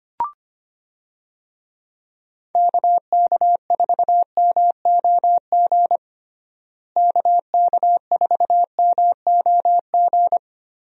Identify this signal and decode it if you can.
Morse code, heard as a keyed tone, about 25 wpm, 700 Hz. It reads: KK4MOG KK4MOG
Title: USA Call Signs - 6 Characters - Repeated 3x - 25wpm